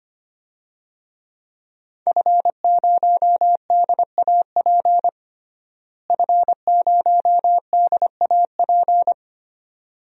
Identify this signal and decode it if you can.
Morse code, heard as a keyed tone, about 25 wpm, 700 Hz. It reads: F0DAP F0DAP
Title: DXCC-100 Call Signs - 5 Characters - Repeated 2x - 25wpm